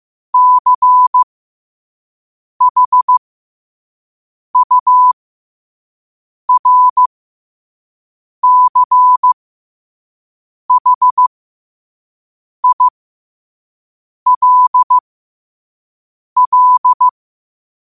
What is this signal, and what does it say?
Morse code, heard as a keyed tone, about 15 wpm, spaced as for 5 wpm, 1000 Hz. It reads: CHURCHILL
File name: qotdfiles / 5wpm